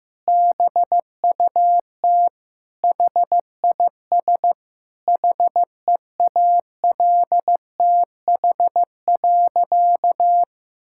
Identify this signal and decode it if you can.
Morse code, heard as a keyed tone, about 15 wpm, 700 Hz. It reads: BUT HIS HEALTH.
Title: Sentences from Top 1000 Words - Review Part 6 - 15wpm